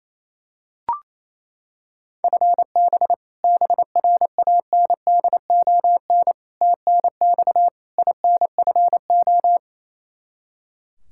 Morse code, heard as a keyed tone, about 28 wpm, 700 Hz. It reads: FB BRANDON TNX INFO
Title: Practice QSO - Exchange 3 - More Info - 28wpm